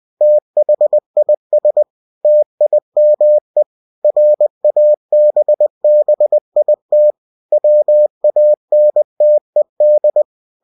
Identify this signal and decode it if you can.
Morse code, heard as a keyed tone, about 20 wpm, 600 Hz. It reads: THIS TIME RABBIT WANTED
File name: cw-books / winnie-the-pooh / winnie-20wpm-page-0016